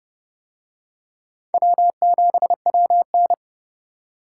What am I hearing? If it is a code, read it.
Morse code, heard as a keyed tone, about 29 wpm, 700 Hz. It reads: W7WN